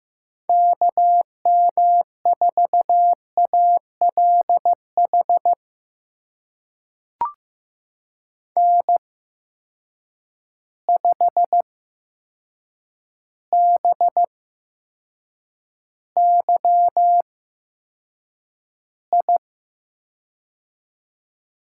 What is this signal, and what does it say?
Morse code, heard as a keyed tone, about 15 wpm, spaced as for 7 wpm, 700 Hz. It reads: KM4ALH N5BYI
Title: USA Call Signs - 8x Character Spacing - 15wpm